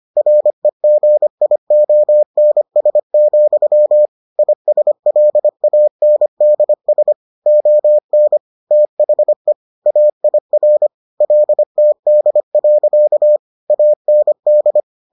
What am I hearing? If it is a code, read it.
Morse code, heard as a keyed tone, about 25 wpm, 600 Hz. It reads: REGIONS, ISLANDS ON THE AIR LTD. AND